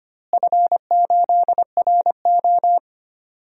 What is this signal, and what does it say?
Morse code, heard as a keyed tone, about 25 wpm, 700 Hz. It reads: F8RO
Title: DXCC-100 Call Signs - 4 Characters - Repeated 3x - 25wpm